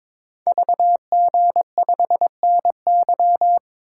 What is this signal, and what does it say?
Morse code, heard as a keyed tone, about 22 wpm, 700 Hz. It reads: VG5NY
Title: Code Groups - 5 Letter:Numbers - 22wpm